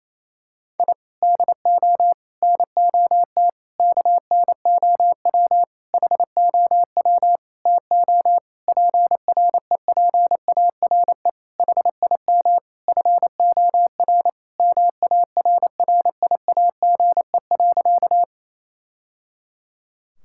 Morse code, heard as a keyed tone, about 28 wpm, 700 Hz. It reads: I DO NOT KNOW HOW TO PREPARE HIM FOR MARRIAGE.
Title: Sentences from Top 1000 Words - Review Part 6 - 28wpm